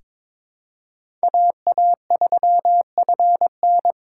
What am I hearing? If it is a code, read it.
Morse code, heard as a keyed tone, about 22 wpm, 700 Hz. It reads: AA3FN
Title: USA Call Signs - 5 Characters - 3x Character Spacing - 22wpm